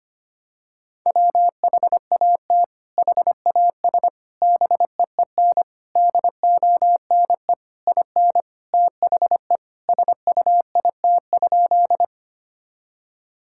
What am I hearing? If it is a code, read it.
Morse code, heard as a keyed tone, about 25 wpm, 700 Hz. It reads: WHAT HAS BEEN DONE IN THE SUIT?
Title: Sentences from Top 1600 Words 25wpm